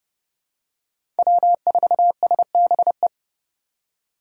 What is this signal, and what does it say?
Morse code, heard as a keyed tone, about 32 wpm, 700 Hz. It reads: W4SBE
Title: USA Call Signs - 5 Characters - 4x Character Spacing - 30wpm